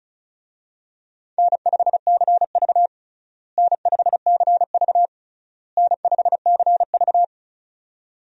Morse code, heard as a keyed tone, about 36 wpm, 700 Hz. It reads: N5CV N5CV N5CV
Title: USA Call Signs - 4 Characters - Repeated 3x - 35wpm